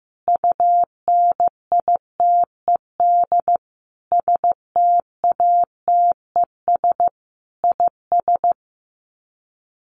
Morse code, heard as a keyed tone, about 15 wpm, 700 Hz. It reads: UNITED STATES IS
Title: Sets of 5 Words Encore2 15wpm